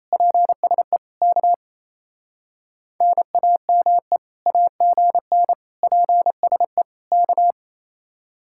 Morse code, heard as a keyed tone, about 33 wpm, 700 Hz. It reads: PSE K NAME AGN PSE K
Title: Speed-Racing Common QSO Phrases - 28wpm